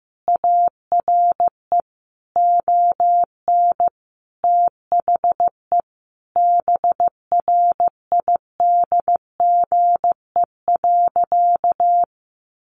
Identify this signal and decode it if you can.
Morse code, heard as a keyed tone, about 15 wpm, 700 Hz. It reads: ARE ON THE BRIDGE.
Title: Sentences from Top 1200 Words 15wpm